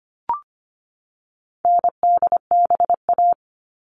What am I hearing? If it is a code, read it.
Morse code, heard as a keyed tone, about 25 wpm, 700 Hz. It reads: NDBA
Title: All Cut Numbers - 4-Digit 25wpm